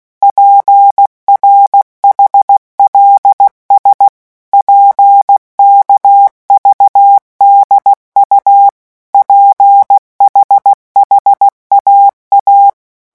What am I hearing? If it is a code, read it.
Morse code, heard as a keyed tone, about 16 wpm, 800 Hz. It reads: PRHLS PKVDU PHHAA
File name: morse mp3 / revision5-vitesse 15 mots